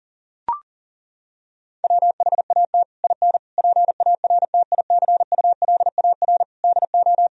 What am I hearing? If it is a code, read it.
Morse code, heard as a keyed tone, about 40 wpm, 700 Hz. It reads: WHAT IN PARTICULAR DO
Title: Sentences from Top 1000 Words - Review Part 4 - 40wpm